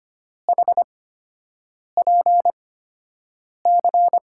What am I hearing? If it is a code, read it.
Morse code, heard as a keyed tone, about 25 wpm, spaced as for 7 wpm, 700 Hz. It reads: HPC
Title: USA Call Signs - 5 Characters - 8x Character Spacing - 25wpm